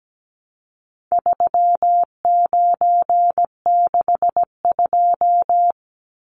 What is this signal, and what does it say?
Morse code, heard as a keyed tone, about 17 wpm, 700 Hz. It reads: 3962
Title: Numbers - 4-Digit 17wpm